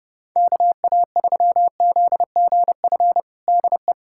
Morse code, heard as a keyed tone, about 30 wpm, 700 Hz. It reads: KA3ZGF DE
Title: USA Callsign DE Callsign 30wpm